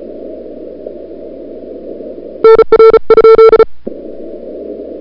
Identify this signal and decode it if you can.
Morse code, heard as a keyed tone, about 36 wpm, 440 Hz. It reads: NR?
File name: CWSprint / February 2007 CW Sprint / 01-2007 CW Sprint Hour1